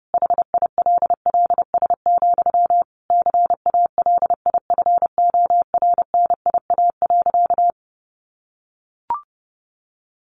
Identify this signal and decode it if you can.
Morse code, heard as a keyed tone, about 30 wpm, 700 Hz. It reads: HILLS, CALIFORNIA.
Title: Easy Phrases 30wpm